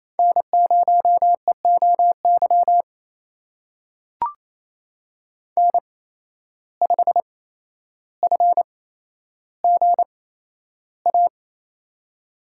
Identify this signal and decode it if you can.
Morse code, heard as a keyed tone, about 28 wpm, spaced as for 12 wpm, 700 Hz. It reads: N0EOY N5FGA
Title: USA Call Signs - 5 Characters - 8x Character Spacing - 28wpm